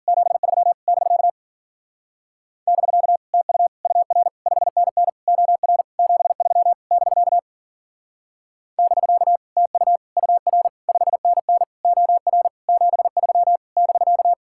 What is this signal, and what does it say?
Morse code, heard as a keyed tone, about 52 wpm, 700 Hz. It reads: 73 <BK> <BK> TU UR 5NN OR 73 <BK> <BK> TU UR 5NN OR 73 <BK>